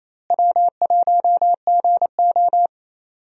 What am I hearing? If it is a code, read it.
Morse code, heard as a keyed tone, about 28 wpm, 700 Hz. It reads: W1GO
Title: USA Call Signs - 4 Characters - Repeated 3x - 28wpm